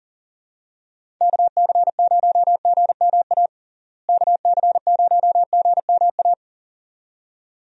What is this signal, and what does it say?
Morse code, heard as a keyed tone, about 40 wpm, 700 Hz. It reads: KC0GMA KC0GMA
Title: USA Call Signs - 6 Characters - Repeated 2x - 40wpm